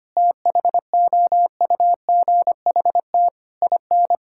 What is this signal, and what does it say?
Morse code, heard as a keyed tone, about 25 wpm, 700 Hz. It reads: THOUGHT IN
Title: Sentences from Top 1000 Words - Review Part 6 - 25wpm